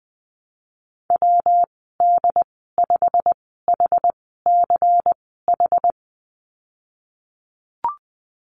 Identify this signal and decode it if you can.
Morse code, heard as a keyed tone, about 20 wpm, spaced as for 14 wpm, 700 Hz. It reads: WD5HCH
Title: 186 Lesson 30g - Letter b - Call Signs 20wpm